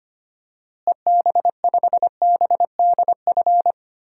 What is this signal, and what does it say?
Morse code, heard as a keyed tone, about 25 wpm, 700 Hz. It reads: EB5BDF
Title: DXCC-100 Call Signs - 6 Characters - Repeated 3x - 25wpm